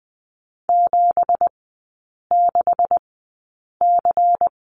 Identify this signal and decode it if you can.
Morse code, heard as a keyed tone, about 20 wpm, spaced as for 8 wpm, 690 Hz. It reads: 76C
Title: 191 Lesson 31e - Letter v - Three Characters 20wpm